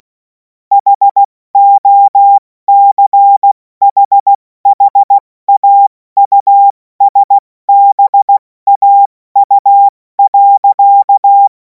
Morse code, heard as a keyed tone, about 16 wpm, spaced as for 14 wpm, 800 Hz. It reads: HOCHHAUSBAU.
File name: Morsekurs / Witze / Witz 07. April 2026 70 F80